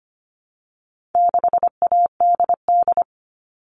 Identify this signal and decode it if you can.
Morse code, heard as a keyed tone, about 25 wpm, 700 Hz. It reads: 6ADD
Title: All Cut Numbers - 4-Digit 25wpm